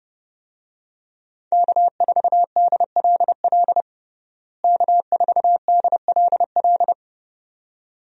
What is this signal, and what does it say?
Morse code, heard as a keyed tone, about 30 wpm, 700 Hz. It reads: K4DLL K4DLL
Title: USA Call Signs - 5 Characters - Repeated 2x - 30wpm